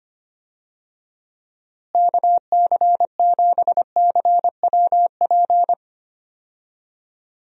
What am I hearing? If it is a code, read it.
Morse code, heard as a keyed tone, about 25 wpm, 700 Hz. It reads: KC7CWP
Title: USA Call Signs - 6 Characters - Repeated 3x - 25wpm